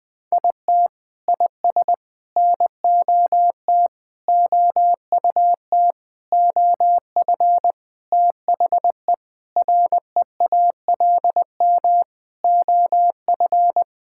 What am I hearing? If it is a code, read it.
Morse code, heard as a keyed tone, about 20 wpm, 700 Hz. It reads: IT IS NOT OUT OF THE REALM OF
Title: Sentences with 9-Word Phrases - 20wpm